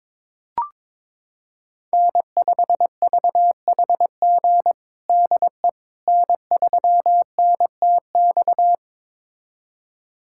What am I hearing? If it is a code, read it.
Morse code, heard as a keyed tone, about 22 wpm, 700 Hz. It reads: N5VHG DE N3NTX
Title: USA Callsign DE Callsign - 5 Characters - 22wpm